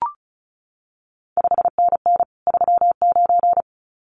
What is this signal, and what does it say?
Morse code, heard as a keyed tone, about 36 wpm, 700 Hz. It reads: 5NN 39
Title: CQ WW DX Contest Exchange Encore - 35wpm